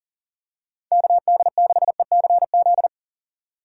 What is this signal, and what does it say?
Morse code, heard as a keyed tone, about 40 wpm, 700 Hz. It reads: KD6ECZ
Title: USA Call Signs - 6x Character Spacing - 40wpm